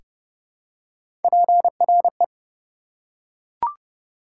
Morse code, heard as a keyed tone, about 31 wpm, 700 Hz. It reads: PRE
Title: Prefixes and Suffixes 30wpm